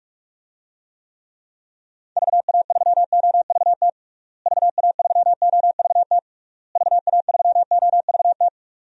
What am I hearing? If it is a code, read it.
Morse code, heard as a keyed tone, about 41 wpm, 700 Hz. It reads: VA3OVT VA3OVT VA3OVT
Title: DXCC-100 Call Signs - 6 Characters - Repeated 3x - 45wpm